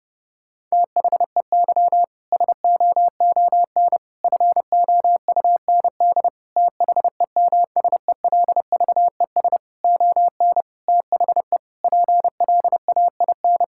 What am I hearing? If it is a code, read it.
Morse code, heard as a keyed tone, about 30 wpm, 700 Hz. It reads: THEY SOON FOUND THEMSELVES ON THE PLAIN
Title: Sentences from Top 1000 Words - Review Part 5 - 30wpm